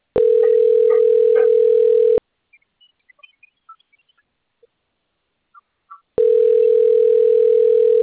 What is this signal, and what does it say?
Ringback tone (North America), 440 + 480 Hz.